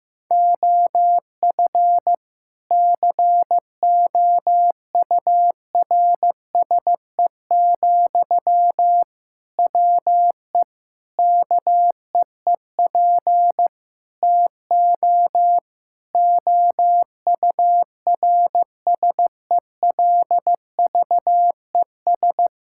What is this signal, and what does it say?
Morse code, heard as a keyed tone, about 15 wpm, 700 Hz. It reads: OF COURSE, WE KEEP TO OURSELVES
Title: Sentences from Top 1600 Words 15wpm